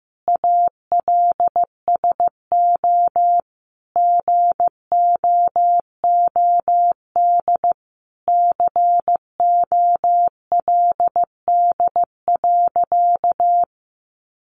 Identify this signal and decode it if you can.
Morse code, heard as a keyed tone, about 15 wpm, 700 Hz. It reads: ALSO GOOD COLD.
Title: Sentences from Top 1300 Words 15wpm